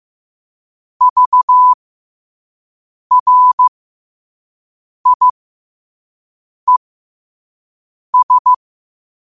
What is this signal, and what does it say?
Morse code, heard as a keyed tone, about 15 wpm, spaced as for 5 wpm, 1000 Hz. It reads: VRIES